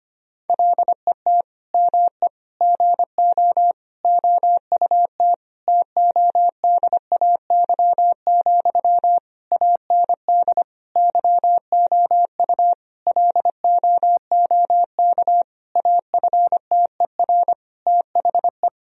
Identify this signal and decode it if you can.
Morse code, heard as a keyed tone, about 25 wpm, 700 Hz. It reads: LET ME GO OUT TODAY, AND YOU LOOK AFTER THE
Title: Sentences from Top 1000 Words - Review Part 4 - 25wpm